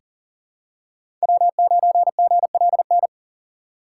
Morse code, heard as a keyed tone, about 40 wpm, 700 Hz. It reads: W9GLN
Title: USA Call Signs - 5 Characters - 3x Character Spacing - 40wpm